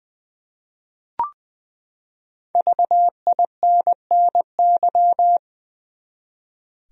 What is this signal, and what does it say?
Morse code, heard as a keyed tone, about 20 wpm, 700 Hz. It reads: VINNY
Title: USA Common Names - Max 5 Letters - 20wpm